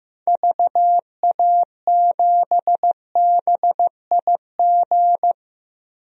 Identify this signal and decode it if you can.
Morse code, heard as a keyed tone, about 16 wpm, 700 Hz. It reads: VA7BIG
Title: DXCC-100 Call Signs - 6 Characters - Repeated 3x - 15wpm